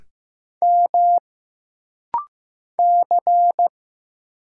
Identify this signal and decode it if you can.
Morse code, heard as a keyed tone, about 15 wpm, 700 Hz. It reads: M C
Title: Single Letter-Number Alt - Rapid-Fire 15wpm